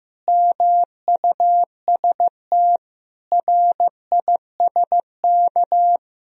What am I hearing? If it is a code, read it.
Morse code, heard as a keyed tone, about 15 wpm, 700 Hz. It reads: MUST RISK
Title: Sentences from Top 1000 Words - Review Part 4 - 15wpm